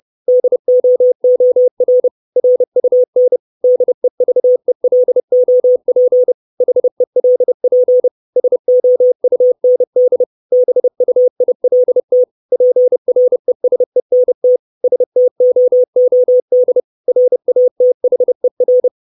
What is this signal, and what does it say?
Morse code, heard as a keyed tone, about 30 wpm, 500 Hz. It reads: DOOR RUN DEVELOP HELP SOUND BUILT PRESENT STOOD RATHER
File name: 632 Words 30WPM